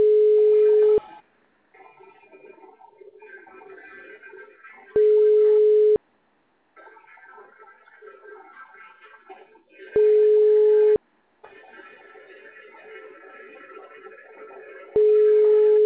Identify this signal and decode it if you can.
Ringback tone (Europe, 425 Hz).